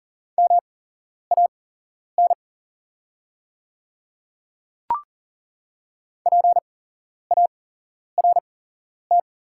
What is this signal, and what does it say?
Morse code, heard as a keyed tone, about 40 wpm, spaced as for 11 wpm, 700 Hz. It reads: MAN PART
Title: 8x Character Spacing - Top 200 Words - 40wpm